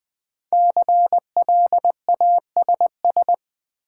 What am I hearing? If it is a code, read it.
Morse code, heard as a keyed tone, about 20 wpm, 700 Hz. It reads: CLASS
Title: Sentences from Top 1000 Words - Review Part 2 - New Word Drill - 20wpm